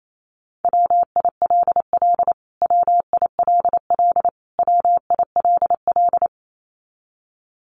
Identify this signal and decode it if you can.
Morse code, heard as a keyed tone, about 28 wpm, 700 Hz. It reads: WILL WILL WILL
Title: Top 200 Words Repeated 28wpm